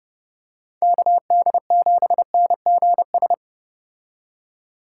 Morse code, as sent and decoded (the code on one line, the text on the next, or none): -.- -.. --... -. --. ...
KD7NGS